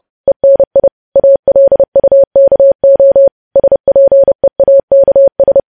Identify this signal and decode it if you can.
Morse code, heard as a keyed tone, about 30 wpm, 550 Hz. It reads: ENI ALUKO SPEAKS
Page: news headlines in morse code at 30 wpm